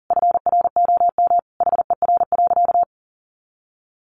Morse code, heard as a keyed tone, about 40 wpm, 700 Hz. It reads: FROM HER.
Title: Sentences from Top 1300 Words 40wpm